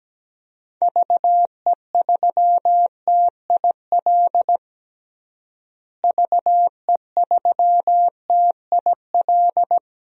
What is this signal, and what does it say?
Morse code, heard as a keyed tone, about 17 wpm, 700 Hz. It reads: VE3TIL VE3TIL
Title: DXCC-100 Call Signs - 6 Characters - Repeated 3x - 17wpm